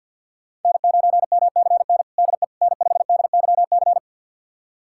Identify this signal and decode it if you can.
Morse code, heard as a keyed tone, about 48 wpm, 700 Hz. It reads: N9MCN DE N5DYC
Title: USA Callsign DE Callsign - 5 Characters - 50wpm